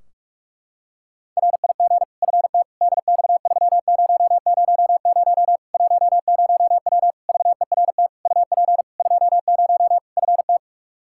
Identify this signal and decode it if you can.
Morse code, heard as a keyed tone, about 46 wpm, 700 Hz. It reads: RIG FT DX3000 10W VERT UP 20 FT
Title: Practice QSO - Exchange 2 - Info - 45wpm